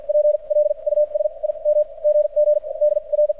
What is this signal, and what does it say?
Morse code, heard as a keyed tone, about 50 wpm, spaced as for 37 wpm, 600 Hz, with radio noise. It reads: PLUS IM GG DR
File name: archived / CRSnet